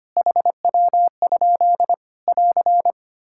Morse code, heard as a keyed tone, about 25 wpm, 700 Hz. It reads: HW? <AR>
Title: Practice QSO - Exchange 3 - More Info - 25wpm